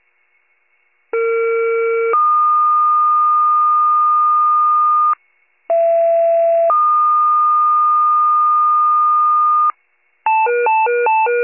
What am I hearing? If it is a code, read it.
Two-tone page, heard as a one-second tone, then a three-second tone, about 460 then 1220 Hz, about 670 then 1220 Hz. It is heard from 1.1 to 9.7 s.
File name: pager / 21 RESCUE / wcalert